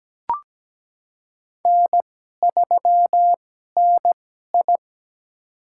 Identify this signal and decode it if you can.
Morse code, heard as a keyed tone, about 17 wpm, spaced as for 12 wpm, 700 Hz. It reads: N3NI
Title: USA Call Signs - 2x Character Spacing - 17wpm